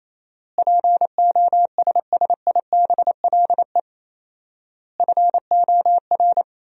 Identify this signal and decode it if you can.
Morse code, heard as a keyed tone, about 28 wpm, 700 Hz. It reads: POSSIBLE FOR
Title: 4x Word-Spacing - Sentences from Top 500 Words - 28wpm